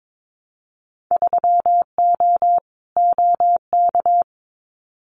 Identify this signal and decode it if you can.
Morse code, heard as a keyed tone, about 22 wpm, 700 Hz. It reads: 3O OK
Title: Winter Field Day Exchanges Encore 22wpm